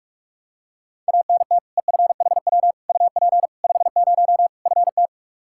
Morse code, heard as a keyed tone, about 42 wpm, 700 Hz. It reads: ANT EFHW UP 50 FT
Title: Practice QSO - Exchange 2 - Info - 45wpm